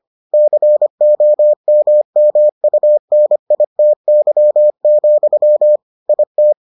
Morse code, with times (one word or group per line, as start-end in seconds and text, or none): 0.3-5.8 COMMUNITY,
6.1-6.5 IT